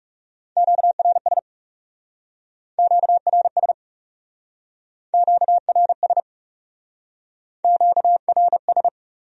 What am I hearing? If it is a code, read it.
Morse code, heard as a keyed tone, about 39 wpm, 700 Hz. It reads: QRS QRS QRS QRS